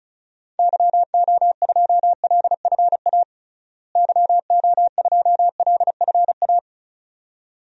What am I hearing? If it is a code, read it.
Morse code, heard as a keyed tone, about 36 wpm, 700 Hz. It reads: YO2LFA YO2LFA
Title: DXCC-100 Call Signs - 6 Characters - Repeated 2x - 35wpm